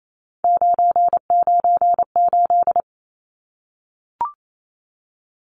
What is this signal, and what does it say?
Morse code, heard as a keyed tone, about 28 wpm, 700 Hz. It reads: 998
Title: Numbers - 3-Digit 28wpm